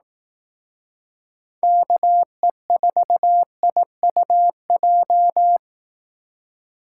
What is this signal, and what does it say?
Morse code, heard as a keyed tone, about 18 wpm, 700 Hz. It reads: KE4IUJ
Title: Speed Racing - USA 2x3 Call Signs 15wpm